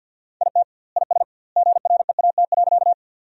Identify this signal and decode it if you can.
Morse code, heard as a keyed tone, about 52 wpm, 700 Hz. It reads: IT IS GREAT.
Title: 2x Word-Spacing - Sentences from Top 200 Words - 50wpm